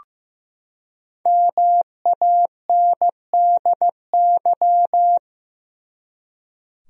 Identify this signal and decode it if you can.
Morse code, heard as a keyed tone, about 15 wpm, 700 Hz. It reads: MANDY
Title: USA Common Names - Max 5 Letters - 15wpm